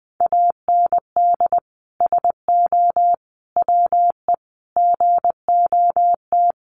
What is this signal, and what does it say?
Morse code, heard as a keyed tone, about 20 wpm, 700 Hz. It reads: AND SO WE GOT
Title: Sentences from Top 1500 Words 20wpm